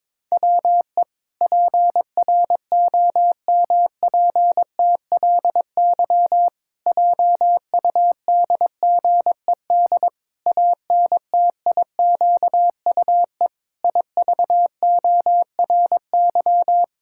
Morse code, heard as a keyed tone, about 22 wpm, 700 Hz. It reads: WE PROMPTLY JUDGED ANTIQUE IVORY
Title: Pangram Challenge v2 - 22wpm